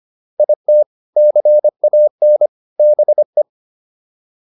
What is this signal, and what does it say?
Morse code, heard as a keyed tone, about 25 wpm, 600 Hz. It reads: IT CAN BE